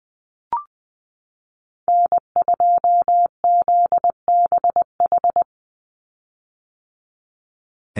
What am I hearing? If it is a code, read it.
Morse code, heard as a keyed tone, about 20 wpm, 700 Hz. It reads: N2ZBH
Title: USA Call Signs 20wpm